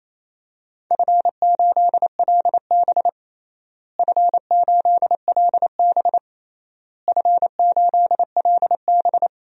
Morse code, heard as a keyed tone, about 28 wpm, 700 Hz. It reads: F8LB F8LB F8LB
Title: DXCC-100 Call Signs - 4 Characters - Repeated 3x - 28wpm